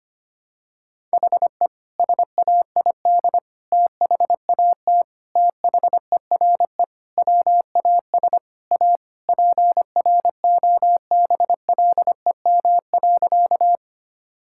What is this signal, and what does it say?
Morse code, heard as a keyed tone, about 25 wpm, 700 Hz. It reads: HE SAID THAT THERE WAS A PROBLEM.
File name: practice / 25wpm / Sentences with 3-Word Phrases - 25wpm